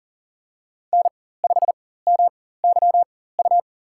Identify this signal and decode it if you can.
Morse code, heard as a keyed tone, about 39 wpm, spaced as for 19 wpm, 700 Hz. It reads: N5MYU
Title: USA Call Signs - 4x Character Spacing - 40wpm